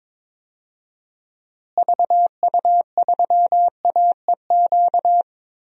Morse code, heard as a keyed tone, about 22 wpm, 700 Hz. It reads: VU3AEQ
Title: DXCC-100 Call Signs - 6 Characters - Repeated 3x - 22wpm